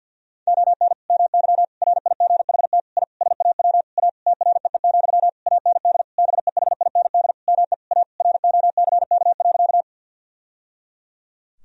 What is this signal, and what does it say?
Morse code, heard as a keyed tone, about 51 wpm, 700 Hz. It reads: ON MY RIGHT I SAW A TREE, AND BEHIND ME A ROCK.
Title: Sentences from Top 1000 Words - Review Part 2 - 50wpm